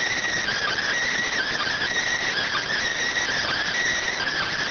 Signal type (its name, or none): Emergency Alert System (SAME) tones